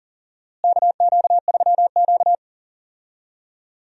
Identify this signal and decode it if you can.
Morse code, heard as a keyed tone, about 40 wpm, 700 Hz. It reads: KQ3Q
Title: USA Call Signs - 4 Characters - Repeated 3x - 40wpm